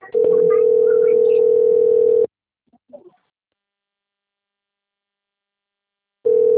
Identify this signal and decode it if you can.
Ringback tone (North America), 440 + 480 Hz.
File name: rg-1600-unknown- - - .57374